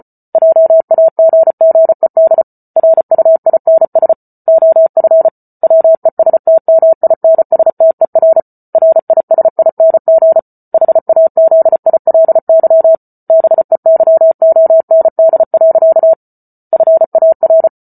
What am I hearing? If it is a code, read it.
Morse code, heard as a keyed tone, about 34 wpm, 650 Hz. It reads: JAGGED RUINS OF WESTMINSTER RISING HAZILY BEYOND. FAR